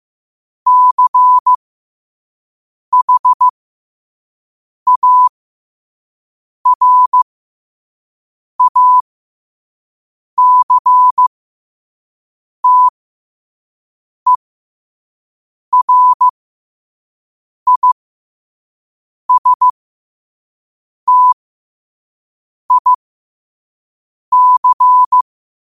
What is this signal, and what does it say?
Morse code, heard as a keyed tone, about 15 wpm, spaced as for 5 wpm, 1000 Hz. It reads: CHARACTERISTIC
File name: qotdfiles / 5wpm